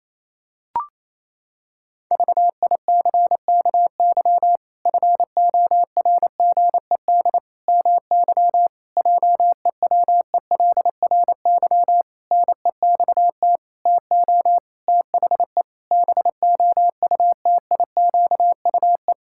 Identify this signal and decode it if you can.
Morse code, heard as a keyed tone, about 28 wpm, 700 Hz. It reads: VICKY FORGED MY JEWELRY NEXT TO THE BOUTIQUE
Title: Pangram Challenge v2 - 28wpm